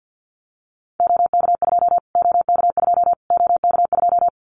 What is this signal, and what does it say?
Morse code, heard as a keyed tone, about 50 wpm, 700 Hz. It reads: OK2 OK2 OK2
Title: Callsign Prefixes for Top 25 DXCC - Repeated 3x - 50wpm